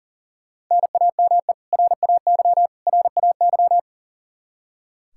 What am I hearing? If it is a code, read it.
Morse code, heard as a keyed tone, about 40 wpm, 700 Hz. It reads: NAME RAY RAY
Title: Practice QSO - Exchange 1 - Beginning - 40wpm